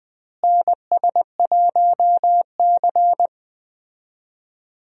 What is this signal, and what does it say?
Morse code, heard as a keyed tone, about 20 wpm, 700 Hz. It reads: NS1C
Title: USA Call Signs - 4 Characters - Repeated 3x - 20wpm